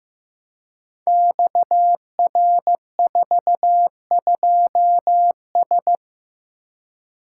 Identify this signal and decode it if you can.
Morse code, heard as a keyed tone, about 15 wpm, 700 Hz. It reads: XR42S
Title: Code Groups - 5 Letter:Numbers - 15wpm